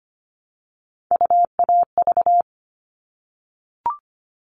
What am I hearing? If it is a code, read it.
Morse code, heard as a keyed tone, about 25 wpm, 700 Hz. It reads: UAV
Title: All Cut Numbers - 3-Digit 25wpm